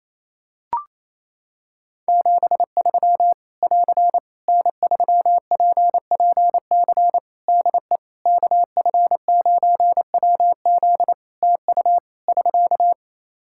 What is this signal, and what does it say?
Morse code, heard as a keyed tone, about 27 wpm, 700 Hz. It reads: 73 <AR> N3PPC DE KF9WZ TU <SK>
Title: K1USN SST QSO Practice 28wpm